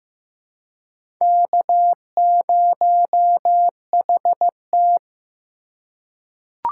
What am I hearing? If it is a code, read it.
Morse code, heard as a keyed tone, about 15 wpm, 700 Hz. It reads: K0HT
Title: K1USN SST QSO Practice - 15wpm